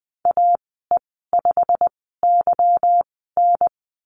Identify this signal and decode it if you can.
Morse code, heard as a keyed tone, about 20 wpm, spaced as for 15 wpm, 700 Hz. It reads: AE5YN
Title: 173 Lesson 28g - Number 9 - Call Signs 20wpm